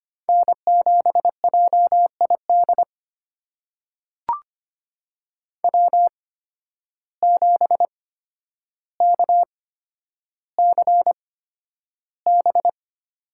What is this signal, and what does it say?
Morse code, heard as a keyed tone, about 25 wpm, spaced as for 11 wpm, 700 Hz. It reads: N7JID W7KCB
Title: USA Call Signs - 5 Characters - 8x Character Spacing - 25wpm